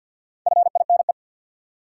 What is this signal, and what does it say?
Morse code, heard as a keyed tone, about 46 wpm, 700 Hz. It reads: FINE
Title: Four Letter Words - 50wpm